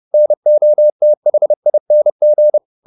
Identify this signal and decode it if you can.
Morse code, heard as a keyed tone, about 30 wpm, 600 Hz. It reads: NOTHING
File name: the old man and the sea / the old man and the sea 1 30wpm